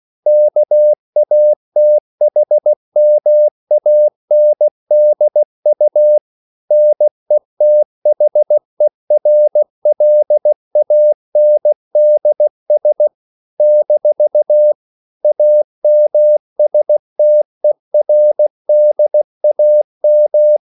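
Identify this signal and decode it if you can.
Morse code, heard as a keyed tone, about 16 wpm, 600 Hz. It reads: KATHMANDU NETHERLANDS - AMSTERDAM
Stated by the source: CW practice of national capitals at 16 words per minute, from ON6ZQ.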